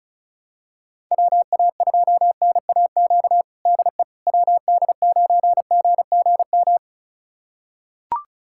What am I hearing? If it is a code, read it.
Morse code, heard as a keyed tone, about 36 wpm, 700 Hz. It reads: WA2NAQ DE WD9GGM
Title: USA Callsign DE Callsign - 6 Characters - 35wpm